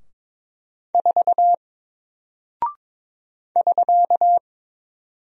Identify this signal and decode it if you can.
Morse code, heard as a keyed tone, about 22 wpm, 700 Hz. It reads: 4 <SK>
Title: Single Character (Letter:Number:Punctuation:Prosign) Alt 22wpm